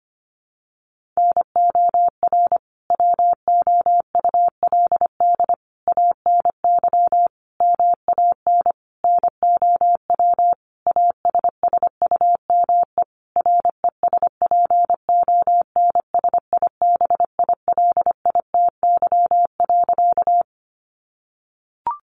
Morse code, heard as a keyed tone, about 25 wpm, 700 Hz. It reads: NOR WOULD ANY MAN NOW ASSUME RESPONSIBILITY.
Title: Sentences from Top 1500 Words 25wpm